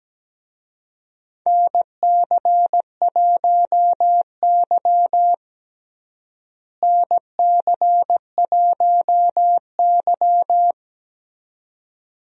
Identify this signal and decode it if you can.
Morse code, heard as a keyed tone, about 17 wpm, 700 Hz. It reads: NC1Y NC1Y
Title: USA Call Signs - 4 Characters - Repeated 2x - 17wpm